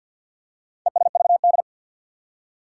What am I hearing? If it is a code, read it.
Morse code, heard as a keyed tone, about 53 wpm, 700 Hz. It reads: ESVD
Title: Four Challenging Letter-Number Combinations 50wpm